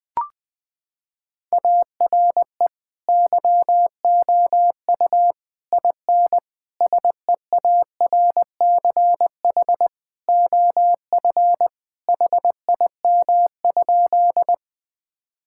Morse code, heard as a keyed tone, about 20 wpm, 700 Hz. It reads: ARE YOU IN SEARCH OF HIM?
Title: Sentences from Top 1000 Words - Review Part 5 - 20wpm